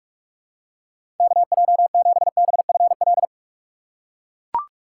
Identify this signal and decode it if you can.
Morse code, heard as a keyed tone, about 44 wpm, 700 Hz. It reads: KJ7BFL
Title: USA Call Signs - 6 Characters - 6x Character Spacing - 45wpm